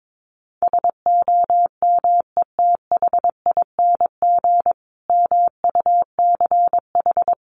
Morse code, heard as a keyed tone, about 22 wpm, 700 Hz. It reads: SOMETHING MUCH
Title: A Sound Of Thunder 22wpm